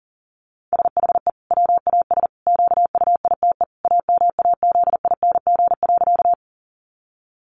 Morse code, heard as a keyed tone, about 40 wpm, 700 Hz. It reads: SHE WAS QUITE AMAZING.
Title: Sentences from Top 1500 Words 40wpm